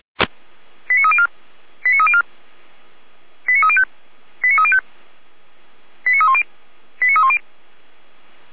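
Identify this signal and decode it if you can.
Five-tone selective call (ZVEI), heard as quick runs of five tones, tones of about 70 ms. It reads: 89304 89304 89305 89305 89310 89310
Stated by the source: ZVEI five-tone selective call received with an SDR, logged on the Feuerwehr Orlamünde audio server.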